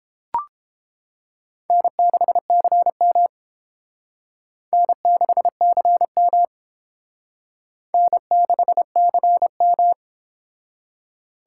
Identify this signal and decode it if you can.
Morse code, heard as a keyed tone, about 29 wpm, 700 Hz. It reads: N6CM N6CM N6CM